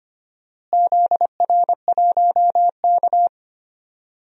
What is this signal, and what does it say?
Morse code, heard as a keyed tone, about 25 wpm, 700 Hz. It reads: ZR1K
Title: DXCC-100 Call Signs - 4 Characters - Repeated 2x - 25wpm